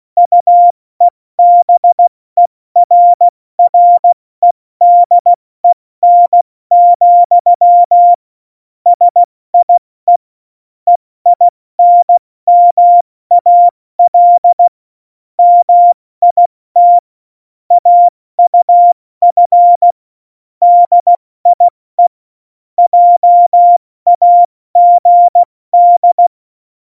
Morse code, heard as a keyed tone, about 16 wpm, 700 Hz. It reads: UEBERREDEN, SIE EINMAL MIT AUF DIE JAGD